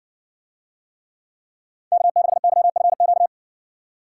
Morse code, heard as a keyed tone, about 60 wpm, 700 Hz. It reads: K6YFQ